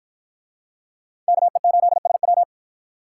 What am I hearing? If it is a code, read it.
Morse code, heard as a keyed tone, about 49 wpm, 700 Hz. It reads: KE8SW